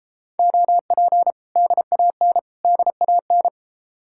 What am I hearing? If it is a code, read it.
Morse code, heard as a keyed tone, about 30 wpm, 700 Hz. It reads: OP DAN DAN